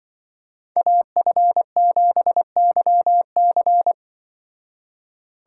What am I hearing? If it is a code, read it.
Morse code, heard as a keyed tone, about 25 wpm, 700 Hz. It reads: AF7YC
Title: Speed Racing - USA 2x2 Call Signs 20wpm